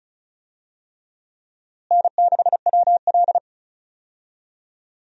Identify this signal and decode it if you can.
Morse code, heard as a keyed tone, about 36 wpm, 700 Hz. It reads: N6WL
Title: USA Call Signs - 4 Characters - Repeated 3x - 35wpm